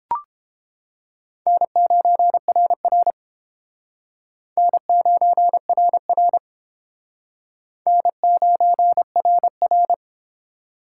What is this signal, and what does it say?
Morse code, heard as a keyed tone, about 29 wpm, 700 Hz. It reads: N9RR N9RR N9RR